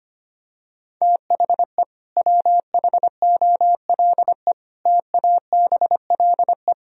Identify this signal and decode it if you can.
Morse code, heard as a keyed tone, about 25 wpm, 700 Hz. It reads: THE WHOLE TABLE
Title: Sentences from Top 1000 Words - Review Part 2 - 25wpm